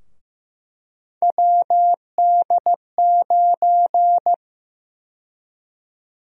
Morse code, heard as a keyed tone, about 15 wpm, 700 Hz. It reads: WD9
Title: USA Call Sign Prefixes - Repeated 2x - 15wpm